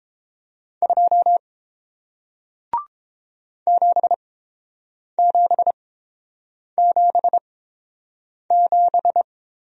Morse code, heard as a keyed tone, about 30 wpm, spaced as for 7 wpm, 700 Hz. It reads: 2 7777